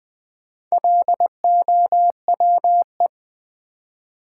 Morse code, heard as a keyed tone, about 20 wpm, 700 Hz. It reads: LOWE
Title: Build A Word - 1 Letter At A Time - 5-Letter Words - 20wpm